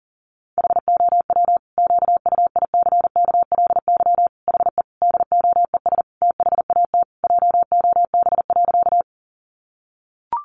Morse code, heard as a keyed tone, about 40 wpm, 700 Hz. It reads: HOW QUICKLY HE DOES THAT JOB.
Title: Sentences from Top 700 Words 40wpm